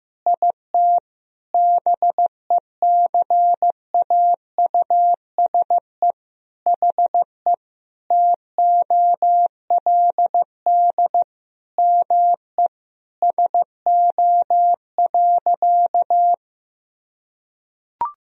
Morse code, heard as a keyed tone, about 15 wpm, 700 Hz. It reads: IT BECAUSE HE TOLD ME SO.
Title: Sentences from Top 1000 Words - Review Part 3 - 15wpm